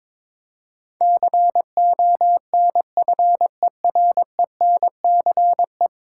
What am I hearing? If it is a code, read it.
Morse code, heard as a keyed tone, about 22 wpm, 700 Hz. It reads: CONFERENCE
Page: morse-code-ninja-practice-22wpm